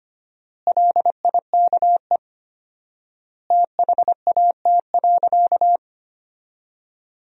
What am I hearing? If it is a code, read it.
Morse code, heard as a keyed tone, about 25 wpm, 700 Hz. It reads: LIKE THAT.
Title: 4x Word-Spacing - Sentences from Top 300 Words - 25wpm